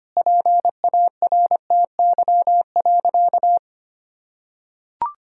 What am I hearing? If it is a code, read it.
Morse code, heard as a keyed tone, about 25 wpm, 700 Hz. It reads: PARTY.
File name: practice / 25wpm / Difficult Sentences with Hints - 25wpm